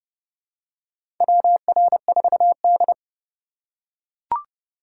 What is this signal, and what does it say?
Morse code, heard as a keyed tone, about 31 wpm, 700 Hz. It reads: WR4D
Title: USA Call Signs - 4 Characters - 4x Character Spacing - 30wpm